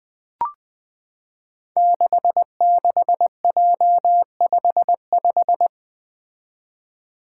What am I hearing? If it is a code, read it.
Morse code, heard as a keyed tone, about 20 wpm, 700 Hz. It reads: 66J55
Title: Five Challenging Letter-Number Combinations 20wpm